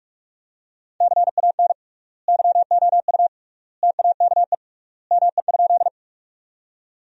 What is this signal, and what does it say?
Morse code, heard as a keyed tone, about 45 wpm, 700 Hz. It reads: CAN YOU TAKE ME?
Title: 3x Word-Spacing - Sentences from Top 200 Words - 45wpm